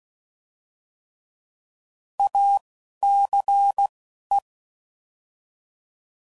Morse code, heard as a keyed tone, about 16 wpm, spaced as for 11 wpm, 800 Hz. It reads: ACE
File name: Prefixes / By First Letter / As